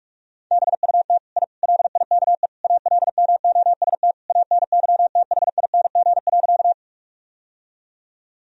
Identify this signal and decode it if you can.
Morse code, heard as a keyed tone, about 41 wpm, 700 Hz. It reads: BUT I LIKE ALMOST ANYTHING.
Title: Sentences from Top 1000 Words - Review Part 4 - 45wpm